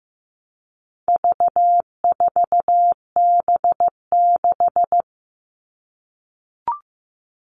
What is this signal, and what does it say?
Morse code, heard as a keyed tone, about 15 wpm, 700 Hz. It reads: V4B6
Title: All Cut Numbers - 4-Digit 15wpm